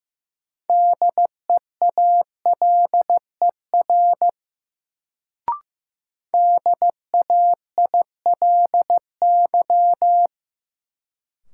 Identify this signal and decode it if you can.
Morse code, heard as a keyed tone, about 15 wpm, 700 Hz. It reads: DEALER DAILY
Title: Sentences from Top 1100 Words - New Word Drill - 15wpm